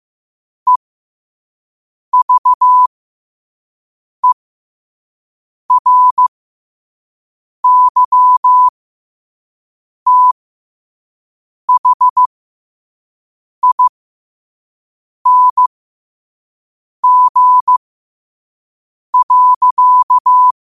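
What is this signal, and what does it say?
Morse code, heard as a keyed tone, about 15 wpm, spaced as for 5 wpm, 1000 Hz. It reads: EVERYTHING.
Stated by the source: Morse at 5 wpm.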